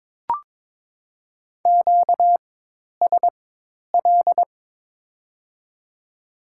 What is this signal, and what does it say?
Morse code, heard as a keyed tone, about 22 wpm, spaced as for 10 wpm, 700 Hz. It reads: QSL
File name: Top 90 QSO Elements - 4x Character Spacing - 22wpm